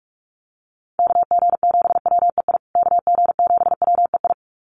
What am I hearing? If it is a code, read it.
Morse code, heard as a keyed tone, about 44 wpm, 700 Hz. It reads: KG7WEI KG7WEI
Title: USA Call Signs Repeated 2x 45wpm